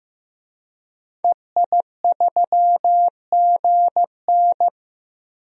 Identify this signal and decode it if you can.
Morse code, heard as a keyed tone, about 15 wpm, 700 Hz. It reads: EI3GN
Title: DXCC-100 Call Signs - 5 Characters - 15wpm